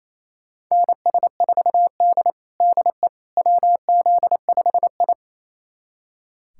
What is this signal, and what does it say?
Morse code, heard as a keyed tone, about 28 wpm, 700 Hz. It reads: NS4D DE WZ5I
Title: USA Callsign DE Callsign - 4 Characters - 28wpm